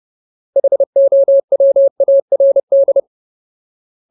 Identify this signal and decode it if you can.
Morse code, heard as a keyed tone, about 30 wpm, 550 Hz. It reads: HOWARD
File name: Ultimate Word List / 30WPM Ultimate